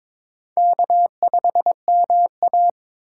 Morse code, heard as a keyed tone, about 22 wpm, 700 Hz. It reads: K5MA
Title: K1USN SST QSO Practice - 22wpm